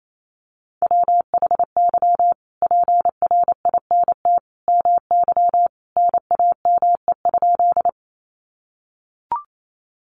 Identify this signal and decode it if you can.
Morse code, heard as a keyed tone, about 28 wpm, 700 Hz. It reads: WHY PRINT MY NAME?
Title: Sentences from Top 1400 Words 28wpm